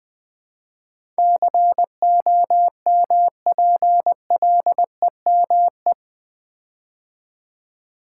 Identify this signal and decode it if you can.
Morse code, heard as a keyed tone, about 20 wpm, 700 Hz. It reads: COMPLEME
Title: Build A Word - 4 Letters At A Time - 10-Letter Words - 20wpm